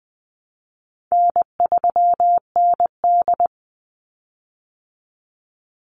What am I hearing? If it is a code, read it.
Morse code, heard as a keyed tone, about 20 wpm, 700 Hz. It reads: N3ND
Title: Easy Call Signs Repeated 3x - 20wpm